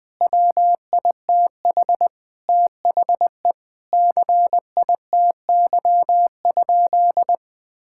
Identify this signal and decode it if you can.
Morse code, heard as a keyed tone, about 21 wpm, 700 Hz. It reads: WITH THE CITY?
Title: Sentences from Top 1600 Words 20wpm